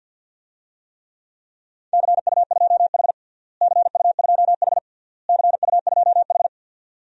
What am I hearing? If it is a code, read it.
Morse code, heard as a keyed tone, about 50 wpm, 700 Hz. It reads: CU2H CU2H CU2H
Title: DXCC-100 Call Signs - 4 Characters - Repeated 3x - 50wpm